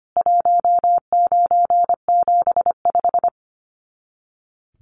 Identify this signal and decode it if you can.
Morse code, heard as a keyed tone, about 25 wpm, 700 Hz. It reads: 1975
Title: Common QSO Phrases 25wpm